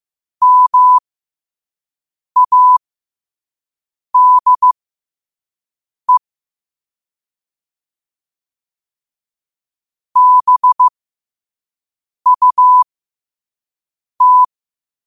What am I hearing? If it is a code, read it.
Morse code, heard as a keyed tone, about 15 wpm, spaced as for 5 wpm, 1000 Hz. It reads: MADE BUT
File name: qotdfiles / 5wpm